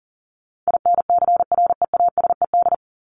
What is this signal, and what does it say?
Morse code, heard as a keyed tone, about 40 wpm, 700 Hz. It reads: INCREASED
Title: ED Suffix - Top 100 Words with Base Words 40wpm